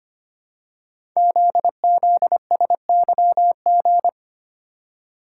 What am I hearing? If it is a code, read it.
Morse code, heard as a keyed tone, about 25 wpm, 700 Hz. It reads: ZZSYG
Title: Code Groups - 5 Letters - 25wpm